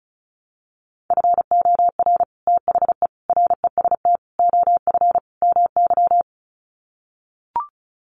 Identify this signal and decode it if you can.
Morse code, heard as a keyed tone, about 34 wpm, 700 Hz. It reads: FOR THE REST OF MY
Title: Sets of 5 Words 35wpm